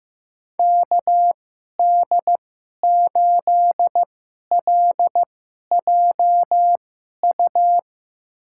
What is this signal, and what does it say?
Morse code, heard as a keyed tone, about 15 wpm, spaced as for 11 wpm, 700 Hz. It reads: KD8LJU
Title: USA Call Signs - 6 Characters - 2x Character Spacing - 15wpm